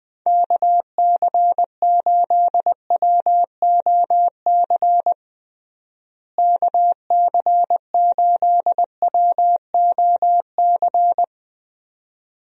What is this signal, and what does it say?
Morse code, heard as a keyed tone, about 20 wpm, 700 Hz. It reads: KC8WOC KC8WOC
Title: USA Call Signs - 6 Characters - Repeated 2x - 20wpm